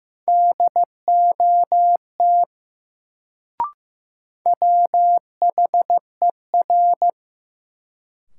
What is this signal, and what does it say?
Morse code, heard as a keyed tone, about 15 wpm, 700 Hz. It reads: DOT WHER
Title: Sentences from Top 1500 Words - New Word Drill Alt - 15wpm